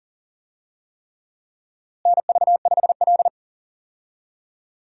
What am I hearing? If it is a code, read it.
Morse code, heard as a keyed tone, about 37 wpm, 700 Hz. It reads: NV5L
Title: USA Call Signs - 4 Characters - Repeated 3x - 40wpm